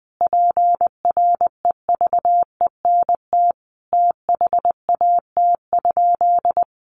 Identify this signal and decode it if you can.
Morse code, heard as a keyed tone, about 19 wpm, 700 Hz. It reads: PREVENT THAT?
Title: Sentences from Top 1100 Words 20wpm